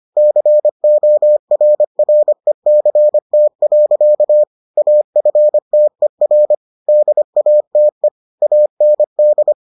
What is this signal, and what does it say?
Morse code, heard as a keyed tone, about 25 wpm, 600 Hz. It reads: CORRECT. AFTER DATE AND